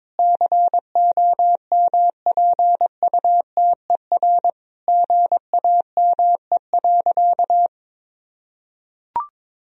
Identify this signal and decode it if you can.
Morse code, heard as a keyed tone, about 22 wpm, 700 Hz. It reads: COMPUTER GAME.
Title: Sentences from Top 1000 Words - Review Part 3 - 22wpm